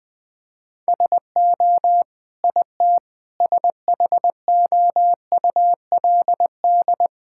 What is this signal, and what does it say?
Morse code, heard as a keyed tone, about 20 wpm, 700 Hz. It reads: SO IT SHOULD
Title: Sets of 9 Words - 20wpm